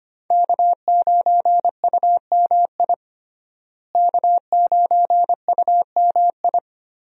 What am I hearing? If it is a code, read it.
Morse code, heard as a keyed tone, about 25 wpm, 700 Hz. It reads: K9UMI K9UMI
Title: USA Call Signs - 5 Characters - Repeated 3x - 25wpm